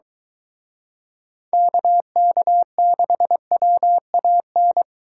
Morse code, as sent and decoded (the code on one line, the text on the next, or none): -.- -.- -.... .-- .- -.
KK6WAN